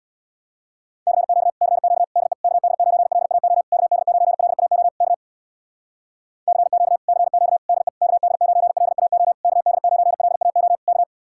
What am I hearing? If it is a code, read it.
Morse code, heard as a keyed tone, about 65 wpm, 700 Hz. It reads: CQ CQ DE KD9LAO KD9LAO K CQ CQ DE KD9LAO KD9LAO K